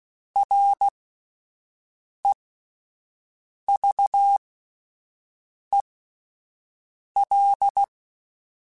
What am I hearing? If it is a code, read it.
Morse code, heard as a keyed tone, about 16 wpm, spaced as for 5 wpm, 800 Hz. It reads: REVEL